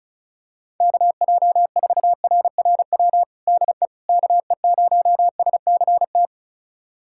Morse code, heard as a keyed tone, about 35 wpm, 700 Hz. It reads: KJ4RRW DE KE0SCT
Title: USA Callsign DE Callsign - 6 Characters - 35wpm